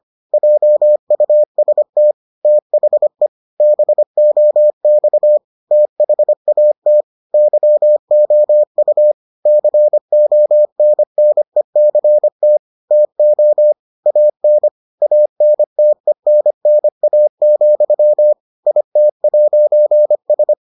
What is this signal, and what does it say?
Morse code, heard as a keyed tone, about 25 wpm, 600 Hz. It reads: JUST THE BOX THAT YOU CONNECT TO AN ANTENNA, IT'S